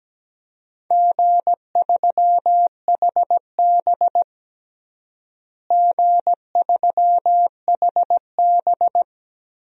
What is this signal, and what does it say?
Morse code, heard as a keyed tone, about 17 wpm, 700 Hz. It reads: G3HB G3HB
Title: DXCC-100 Call Signs - 4 Characters - Repeated 2x - 17wpm